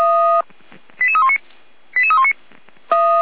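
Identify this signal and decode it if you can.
Five-tone selective call (ZVEI), heard as quick runs of five tones, tones of about 70 ms. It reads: 88319 88319
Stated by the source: Five-tone ZVEI selective call received on an SDR.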